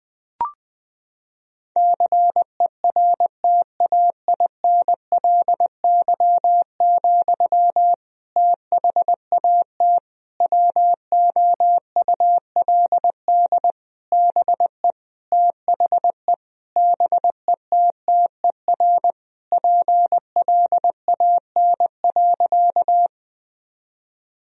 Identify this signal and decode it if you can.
Morse code, heard as a keyed tone, about 20 wpm, 700 Hz. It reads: CERTAINLY, THAT WOULD BE THE BETTER PLAN.
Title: Sentences from Top 1000 Words - Review Part 2 - 20wpm